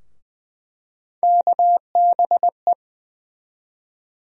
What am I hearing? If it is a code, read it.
Morse code, heard as a keyed tone, about 20 wpm, 700 Hz. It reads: KBE
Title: USA Call Sign Postfixes - 3 Letters 20wpm